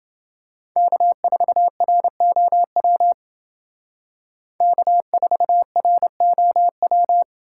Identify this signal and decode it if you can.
Morse code, heard as a keyed tone, about 29 wpm, 700 Hz. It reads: K4ROW K4ROW